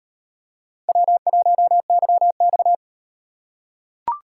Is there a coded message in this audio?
Yes: Morse code from 0.9 to 2.8 s, heard as a keyed tone, about 35 wpm, 700 Hz. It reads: W1YX